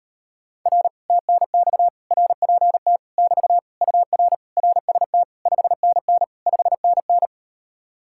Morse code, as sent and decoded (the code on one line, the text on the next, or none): .-. / - -. -..- / .-. .--. - / -...- / ..- .-. / .-. ... - / ..... -. -. / ..... -. -.
R TNX RPT <BT> UR RST 5NN 5NN